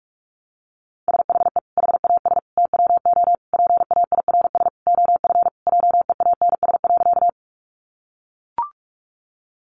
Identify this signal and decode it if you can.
Morse code, heard as a keyed tone, about 44 wpm, 700 Hz. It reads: SHE HAS TWO PAIRS OF JEANS.
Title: Additional Easy Phrases 45wpm